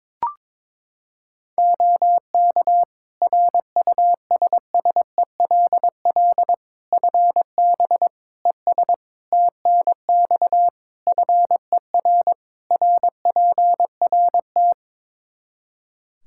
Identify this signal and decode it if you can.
Morse code, heard as a keyed tone, about 22 wpm, 700 Hz. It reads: OK RUSSELL FB ES TNX FER RPRT
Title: Practice QSO - Exchange 2 - Info - 22wpm